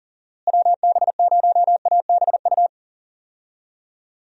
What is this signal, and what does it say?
Morse code, heard as a keyed tone, about 40 wpm, 700 Hz. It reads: WB0ABU
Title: USA Call Signs - 6 Characters - Repeated 3x - 40wpm